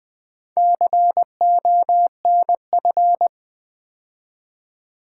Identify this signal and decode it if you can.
Morse code, heard as a keyed tone, about 20 wpm, 700 Hz. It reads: CONF
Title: Build A Word - 4 Letters At A Time - 12-Letter Words - 20wpm